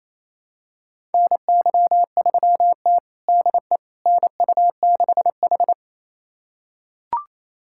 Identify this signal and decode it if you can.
Morse code, heard as a keyed tone, about 28 wpm, 700 Hz. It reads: NY3T DE NU6H
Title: USA Callsign DE Callsign - 4 Characters - 28wpm